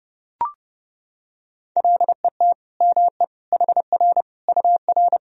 Morse code, heard as a keyed tone, about 30 wpm, 700 Hz. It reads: LET ME HR UR
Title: Sentences from Top 300 Words and Abbreviations 30wpm